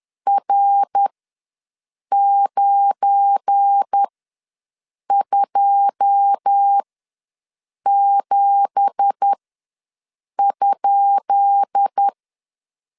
Morse code, heard as a keyed tone, about 10 wpm, spaced as for 6 wpm, 800 Hz. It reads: R927?
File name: LEZIONI / 10 SIGNS / 10B INT ws 10wpm 4134